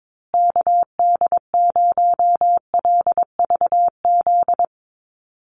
Morse code, heard as a keyed tone, about 22 wpm, 700 Hz. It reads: KD0LVZ
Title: USA Call Signs Repeated 3x 22wpm